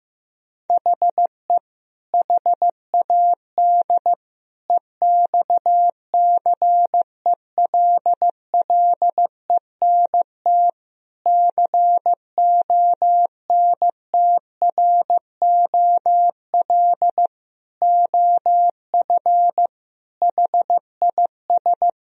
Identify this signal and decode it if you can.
Morse code, heard as a keyed tone, about 15 wpm, 700 Hz. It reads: HE HAD EXCELLENT CONTROL OF HIS
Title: Sentences from Top 1000 Words - Review Part 3 - 15wpm